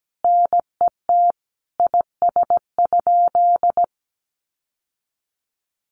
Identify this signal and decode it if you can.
Morse code, heard as a keyed tone, about 17 wpm, 700 Hz. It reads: NET IS?
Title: Sentences from Top 1400 Words 17wpm